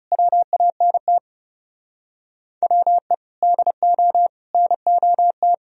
Morse code, heard as a keyed tone, about 36 wpm, 700 Hz. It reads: WANT WE DO NOT
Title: Speed-Racing - Sets of 4 Words 30wpm